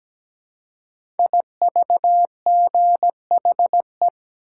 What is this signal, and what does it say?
Morse code, heard as a keyed tone, about 17 wpm, 700 Hz. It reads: IVGHE
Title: Code Groups - 5 Letters - 17wpm